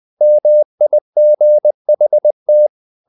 Morse code, heard as a keyed tone, about 20 wpm, 600 Hz. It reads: MIGHT
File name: the old man and the sea part 2 20wpm